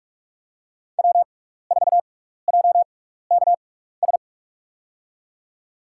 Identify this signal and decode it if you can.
Morse code, heard as a keyed tone, about 44 wpm, 700 Hz. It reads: W 4 J K S
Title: USA Call Signs - 5 Characters - 6x Character Spacing - 45wpm